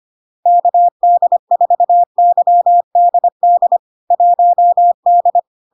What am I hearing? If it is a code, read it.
Morse code, heard as a keyed tone, about 25 wpm, 700 Hz. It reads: KD4YDD 1D
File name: LARK-FD-2022-Log-in-CW-25wpm-700-Hz